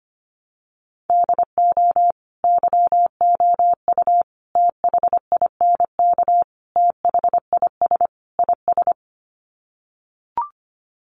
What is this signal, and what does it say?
Morse code, heard as a keyed tone, about 25 wpm, 700 Hz. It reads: DO YOU THINK THIS IS
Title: Sets of 5 Words Encore2 25wpm